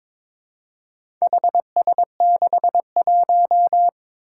Morse code, heard as a keyed tone, about 21 wpm, 700 Hz. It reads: HS61
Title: Four Challenging Letter-Number Combinations 22wpm